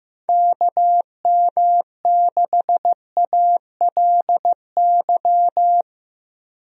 Morse code, heard as a keyed tone, about 15 wpm, 700 Hz. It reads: KM6ALY
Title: USA Call Signs - 6 Characters - Repeated 2x - 15wpm